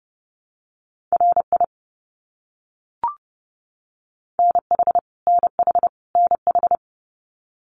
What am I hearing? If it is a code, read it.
Morse code, heard as a keyed tone, about 30 wpm, 700 Hz. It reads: RI NH NH NH
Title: US States Repeated 3x 30wpm